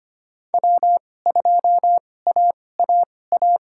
Morse code, heard as a keyed tone, about 25 wpm, spaced as for 18 wpm, 700 Hz. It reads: W2AAA
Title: USA Call Signs - 2x Character Spacing - 25wpm